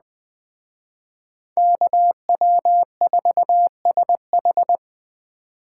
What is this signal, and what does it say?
Morse code, heard as a keyed tone, about 20 wpm, 700 Hz. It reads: KW4SH